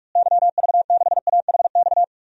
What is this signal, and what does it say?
Morse code, heard as a keyed tone, about 46 wpm, 700 Hz. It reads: YV6AHX
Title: DXCC-100 Call Signs - 6 Characters - Repeated 3x - 45wpm